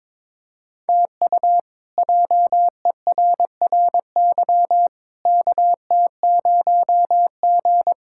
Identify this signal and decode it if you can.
Morse code, heard as a keyed tone, about 22 wpm, 700 Hz. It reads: TU JERRY KT0G
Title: K1USN SST QSO Practice - 22wpm